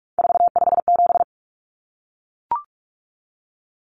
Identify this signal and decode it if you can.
Morse code, heard as a keyed tone, about 44 wpm, 700 Hz. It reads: 457
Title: Common Cut Numbers - 3-Digit 45wpm